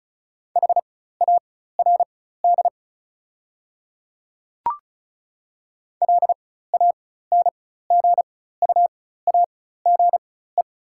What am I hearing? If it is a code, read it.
Morse code, heard as a keyed tone, about 36 wpm, spaced as for 16 wpm, 690 Hz. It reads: HARD LANGUAGE
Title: 4x Character Spacing - Top 500 Words - 35wpm